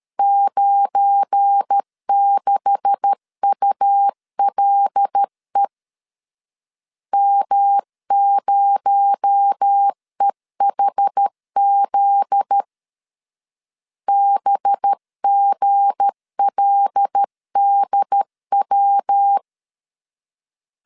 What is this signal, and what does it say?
Morse code, heard as a keyed tone, about 13 wpm, 800 Hz. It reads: 96ULE M0EHZ BGLDW
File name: LEZIONI / 9 286 12wpm 2305